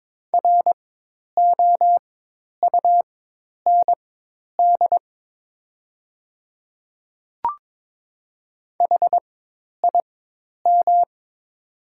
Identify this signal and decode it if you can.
Morse code, heard as a keyed tone, about 22 wpm, spaced as for 10 wpm, 700 Hz. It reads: ROUND HIM
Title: 4x Character Spacing - Top 200 Words - 22wpm